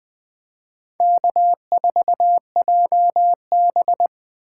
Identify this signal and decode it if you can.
Morse code, heard as a keyed tone, about 20 wpm, 700 Hz. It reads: K4JB
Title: Speed Racing - USA 1x2 Call Signs 17wpm